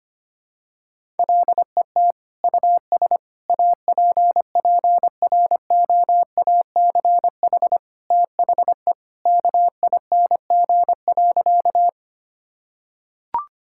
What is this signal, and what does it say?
Morse code, heard as a keyed tone, about 25 wpm, 700 Hz. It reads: LET US APPROACH THE KING.
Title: Sentences from Top 1000 Words - Review Part 6 - 25wpm